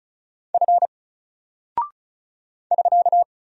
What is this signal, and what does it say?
Morse code, heard as a keyed tone, about 36 wpm, 700 Hz. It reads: F <SK>
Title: Single Character (Letter:Number:Punctuation:Prosign) 35wpm